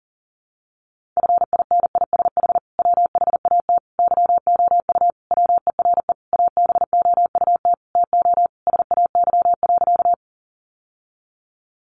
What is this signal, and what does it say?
Morse code, heard as a keyed tone, about 40 wpm, 700 Hz. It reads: FINISH WHAT YOU WERE ABOUT TO SAY.
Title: Sentences from Top 1200 Words 40wpm